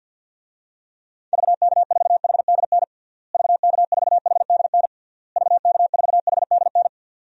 Morse code, heard as a keyed tone, about 52 wpm, 700 Hz. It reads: VK4HDN VK4HDN VK4HDN
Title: DXCC-100 Call Signs - 6 Characters - Repeated 3x - 50wpm